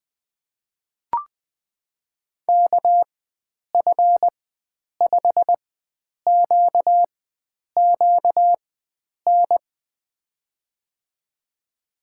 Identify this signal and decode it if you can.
Morse code, heard as a keyed tone, about 20 wpm, spaced as for 9 wpm, 700 Hz. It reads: KF5QQN
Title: USA Call Signs - 6 Characters - 4x Character Spacing - 20wpm